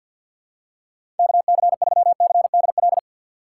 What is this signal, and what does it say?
Morse code, heard as a keyed tone, about 53 wpm, 700 Hz. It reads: KC3CDL
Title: USA Call Signs - 6 Characters - 6x Character Spacing - 50wpm